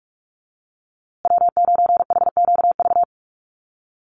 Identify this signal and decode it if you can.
Morse code, heard as a keyed tone, about 44 wpm, 700 Hz. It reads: W9HQV
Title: USA Call Signs Repeated 2x 45wpm